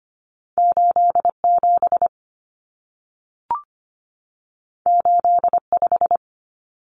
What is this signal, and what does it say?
Morse code, heard as a keyed tone, about 25 wpm, 700 Hz. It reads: 87 85
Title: Common Cut Numbers - 2-Digit 25wpm